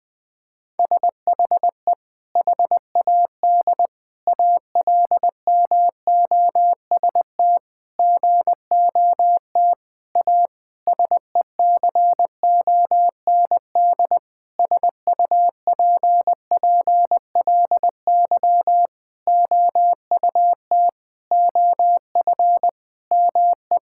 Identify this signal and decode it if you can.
Morse code, heard as a keyed tone, about 20 wpm, 700 Hz. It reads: SHE HAD ALMOST GOT A SECOND SUPPLY OUT OF ME
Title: Sentences from Top 1000 Words - Review Part 5 - 20wpm